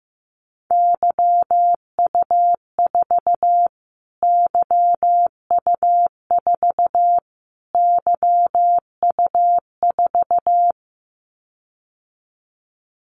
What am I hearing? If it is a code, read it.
Morse code, heard as a keyed tone, about 15 wpm, 700 Hz. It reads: YU4 YU4 YU4
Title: Callsign Prefixes for Top 25 DXCC - Repeated 3x - 15wpm